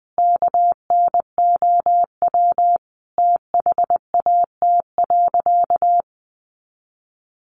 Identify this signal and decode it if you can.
Morse code, heard as a keyed tone, about 20 wpm, 700 Hz. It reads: KNOW THAT.
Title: Sentences from Top 500 Words 20wpm